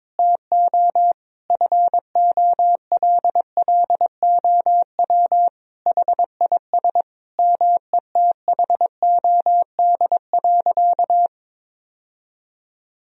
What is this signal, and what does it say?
Morse code, heard as a keyed tone, about 22 wpm, 700 Hz. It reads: TO FOLLOW HIS METHOD.
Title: Sentences from Top 1000 Words - Review Part 3 - 22wpm